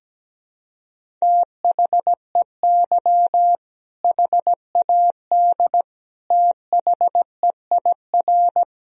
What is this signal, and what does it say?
Morse code, heard as a keyed tone, about 17 wpm, 700 Hz. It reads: THEY HAD THEIR
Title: Sentences from Top 1000 Words - Review Part 3 - 17wpm